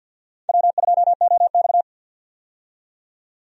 Morse code, heard as a keyed tone, about 52 wpm, 700 Hz. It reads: W2OX
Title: USA Call Signs - 4 Characters - 50wpm